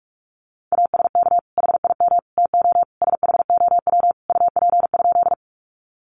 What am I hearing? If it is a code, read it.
Morse code, heard as a keyed tone, about 44 wpm, 700 Hz. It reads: ASK HIM TO SHOW UP?
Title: Sentences from Top 200 Words 45wpm